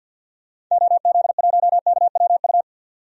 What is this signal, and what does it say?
Morse code, heard as a keyed tone, about 50 wpm, 700 Hz. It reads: OZ1KWU
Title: DXCC-100 Call Signs - 6 Characters - 50wpm